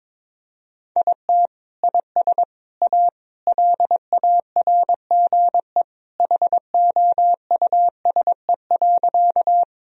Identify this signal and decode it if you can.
Morse code, heard as a keyed tone, about 22 wpm, 700 Hz. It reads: IT IS A LARGE HOUSE.
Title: Sentences from Top 1000 Words - Review Part 1 - 22wpm